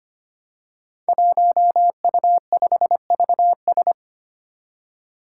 Morse code, heard as a keyed tone, about 25 wpm, 700 Hz. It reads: 1U5VS
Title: Five Challenging Letter-Number Combinations 25wpm